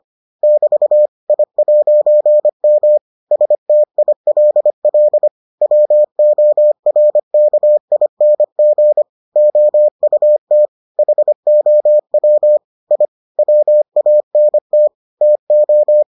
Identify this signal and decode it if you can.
Morse code, heard as a keyed tone, about 25 wpm, 600 Hz. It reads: <BT> I'M STILL WORKING OUT HOW I WANT TO